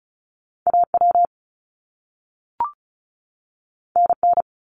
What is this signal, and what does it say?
Morse code, heard as a keyed tone, about 36 wpm, 700 Hz. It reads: AW NN
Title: Top Two Letter Combinations in Words 35wpm